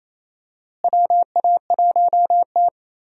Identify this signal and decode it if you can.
Morse code, heard as a keyed tone, about 28 wpm, 700 Hz. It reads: WA1T
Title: USA Call Signs - 4 Characters - Repeated 3x - 28wpm